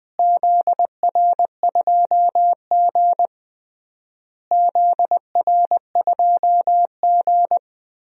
Morse code, heard as a keyed tone, about 20 wpm, 700 Hz. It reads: ZR2G ZR2G
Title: DXCC-100 Call Signs - 4 Characters - Repeated 3x - 20wpm